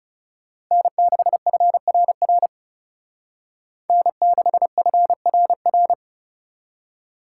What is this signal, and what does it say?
Morse code, heard as a keyed tone, about 37 wpm, 700 Hz. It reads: N6FRR N6FRR
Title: Speed Racing - USA 1x3 Call Signs 30wpm